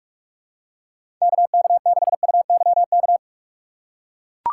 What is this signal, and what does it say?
Morse code, heard as a keyed tone, about 42 wpm, 700 Hz. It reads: KK6UYK
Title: USA Call Signs - 2x Character Spacing - 45wpm